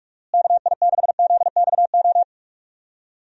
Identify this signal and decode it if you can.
Morse code, heard as a keyed tone, about 41 wpm, 700 Hz. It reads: KI6ZXO
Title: USA Call Signs - 6 Characters - Repeated 3x - 45wpm